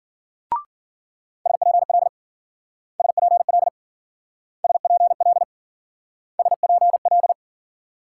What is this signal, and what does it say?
Morse code, heard as a keyed tone, about 54 wpm, 700 Hz. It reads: SPL SPL SPL SPL